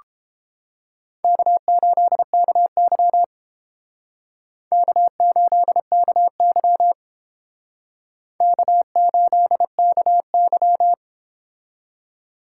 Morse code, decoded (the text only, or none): K8KY K8KY K8KY